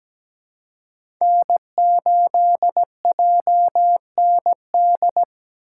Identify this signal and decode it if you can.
Morse code, heard as a keyed tone, about 17 wpm, 700 Hz. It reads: N8JND
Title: USA Call Signs - 5 Characters - 4x Character Spacing - 17wpm